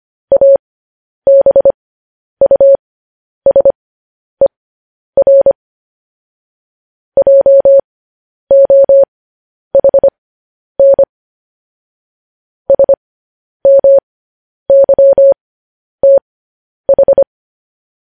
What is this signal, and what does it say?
Morse code, heard as a keyed tone, about 25 wpm, spaced as for 10 wpm, 550 Hz. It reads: ABUSER JOHN SMYTH
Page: news-headlines-in-morse-code-at-10-wpm